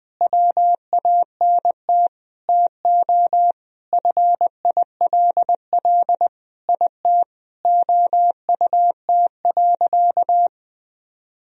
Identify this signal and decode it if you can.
Morse code, heard as a keyed tone, about 20 wpm, 700 Hz. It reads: WANT TO FILL IT OUT.
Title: Sentences from Top 1000 Words - Review Part 3 - 20wpm